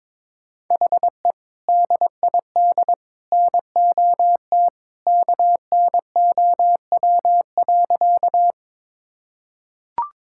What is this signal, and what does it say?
Morse code, heard as a keyed tone, about 22 wpm, 700 Hz. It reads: HE DID NOT KNOW.
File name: practice / 22wpm / Sentences with 2-Word Phrases - 22wpm